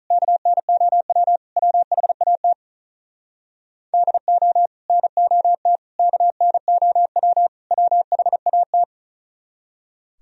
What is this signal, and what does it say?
Morse code, heard as a keyed tone, about 46 wpm, 700 Hz. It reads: KNOW WHAT DO NOT KNOW WHAT